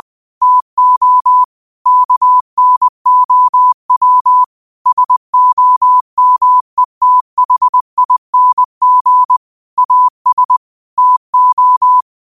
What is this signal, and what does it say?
Morse code, heard as a keyed tone, about 20 wpm, 1000 Hz. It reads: TO KNOW SOMETHING AS TO
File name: qotdfiles / 20wpm